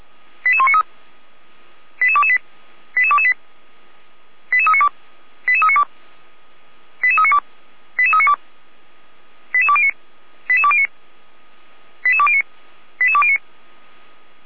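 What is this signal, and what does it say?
Five-tone selective call (ZVEI), heard as quick runs of five tones, tones of about 70 ms. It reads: 88193 88228 88228 88392 88392 88392 88392 88209 88209 88209 88209
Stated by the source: ZVEI five-tone call 88193 received by SDR, logged on the Feuerwehr Orlamünde server.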